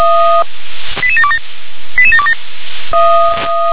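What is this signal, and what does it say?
Five-tone selective call (ZVEI), heard as quick runs of five tones, tones of about 80 ms. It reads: 88727 88727
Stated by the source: ZVEI five-tone call received with an SDR.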